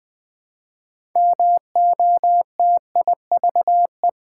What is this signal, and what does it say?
Morse code, heard as a keyed tone, about 20 wpm, 700 Hz. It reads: MOTIVE
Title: Build A Word - 2 Letters At A Time - 6-Letter Words - 20wpm